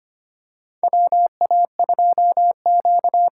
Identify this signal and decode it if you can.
Morse code, heard as a keyed tone, about 25 wpm, 700 Hz. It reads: WA2Q
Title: USA Call Signs - 6 Characters - 6x Character Spacing - 25wpm